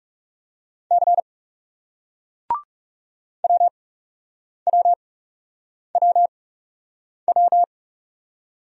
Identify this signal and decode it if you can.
Morse code, heard as a keyed tone, about 40 wpm, spaced as for 8 wpm, 700 Hz. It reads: C WWWW